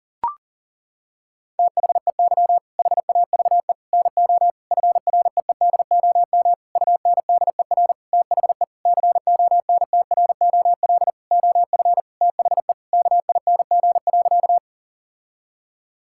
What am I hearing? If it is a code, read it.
Morse code, heard as a keyed tone, about 38 wpm, 700 Hz. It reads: THEY HAVE NO FREEDOM UNDER THE CONTROL OF THE KING.
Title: Sentences from Top 1000 Words - Review Part 5 - 40wpm